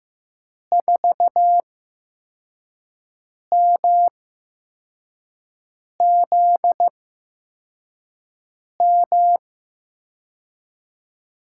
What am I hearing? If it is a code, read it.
Morse code, heard as a keyed tone, about 15 wpm, spaced as for 4 wpm, 700 Hz. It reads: 4MZM
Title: USA Call Signs - 6 Characters - 8x Character Spacing - 15wpm